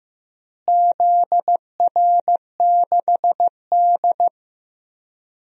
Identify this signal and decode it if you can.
Morse code, heard as a keyed tone, about 15 wpm, 700 Hz. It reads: ZR6D
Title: DXCC-100 Call Signs - 4 Characters - Repeated 2x - 15wpm